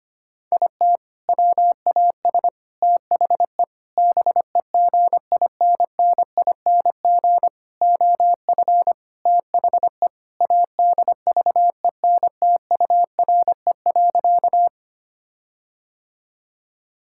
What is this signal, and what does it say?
Morse code, heard as a keyed tone, about 25 wpm, 700 Hz. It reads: IT WAS THE BEGINNING OF THE ADVENTURE.
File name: practice / 25wpm / Sentences with 4-Word Phrases - 25wpm